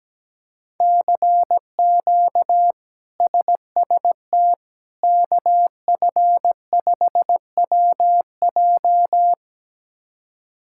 Morse code, heard as a keyed tone, about 17 wpm, 700 Hz. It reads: CQ SST KF5WJ
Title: K1USN SST QSO Practice - 17wpm